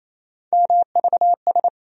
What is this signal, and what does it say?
Morse code, heard as a keyed tone, about 28 wpm, 700 Hz. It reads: MVS
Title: USA Call Sign Postfixes - 3 Letters 28wpm